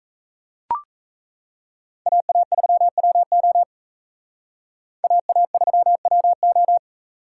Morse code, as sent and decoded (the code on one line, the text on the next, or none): .- .- ...-- .-- --- / .- .- ...-- .-- ---
AA3WO AA3WO